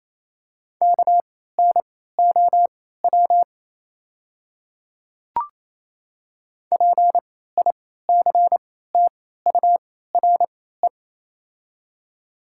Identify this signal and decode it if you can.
Morse code, heard as a keyed tone, about 28 wpm, spaced as for 16 wpm, 700 Hz. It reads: KNOW PICTURE
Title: 3x Character Spacing - Top 200 Words - 28wpm